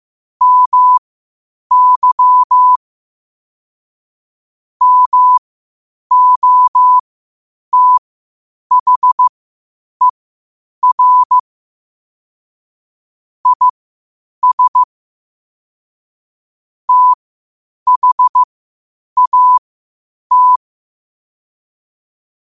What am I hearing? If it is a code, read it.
Morse code, heard as a keyed tone, about 15 wpm, spaced as for 8 wpm, 1000 Hz. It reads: MY MOTHER IS THAT
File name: qotdfiles / 8wpm